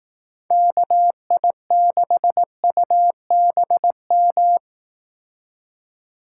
Morse code, heard as a keyed tone, about 18 wpm, 700 Hz. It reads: KI6UBM
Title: Speed Racing - USA 2x3 Call Signs 15wpm